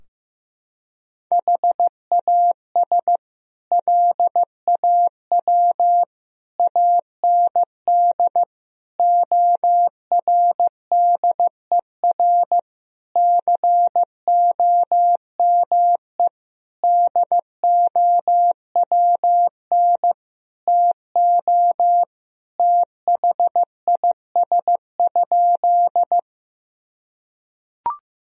Morse code, heard as a keyed tone, about 15 wpm, 700 Hz. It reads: HAS LAW AND ORDER COME DOWN TO THIS?
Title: Sentences from Top 1000 Words - Review Part 3 - 15wpm